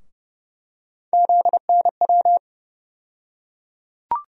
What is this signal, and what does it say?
Morse code, heard as a keyed tone, about 30 wpm, 700 Hz. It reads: ZNW
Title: USA Call Sign Postfixes - 3 Letters 30wpm